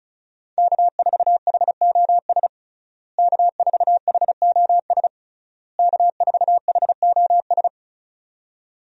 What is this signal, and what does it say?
Morse code, heard as a keyed tone, about 34 wpm, 700 Hz. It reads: K4HOS K4HOS K4HOS
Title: USA Call Signs - 5 Characters - Repeated 3x - 35wpm